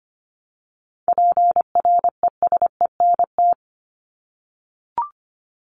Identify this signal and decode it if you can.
Morse code, heard as a keyed tone, about 25 wpm, 700 Hz. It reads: PRESENT
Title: Top 501-750 Words 25wpm